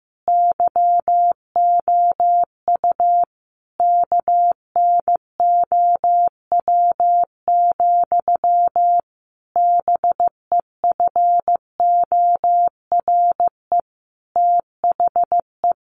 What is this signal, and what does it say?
Morse code, heard as a keyed tone, about 15 wpm, 700 Hz. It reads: YOU KNOW, BEFORE THE
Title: A Sound Of Thunder 15wpm